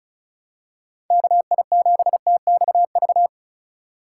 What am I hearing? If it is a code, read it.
Morse code, heard as a keyed tone, about 36 wpm, 700 Hz. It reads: KI7TXV
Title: USA Call Signs - 6 Characters - 3x Character Spacing - 35wpm